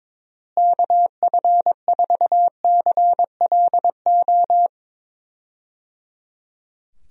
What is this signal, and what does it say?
Morse code, heard as a keyed tone, about 22 wpm, 700 Hz. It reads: KF4CLO
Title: USA Call Signs - 6 Characters - Repeated 3x - 22wpm